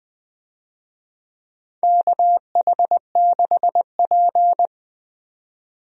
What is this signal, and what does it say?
Morse code, heard as a keyed tone, about 21 wpm, 700 Hz. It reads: KH6P
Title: DXCC-100 Call Signs - 4 Characters - 20wpm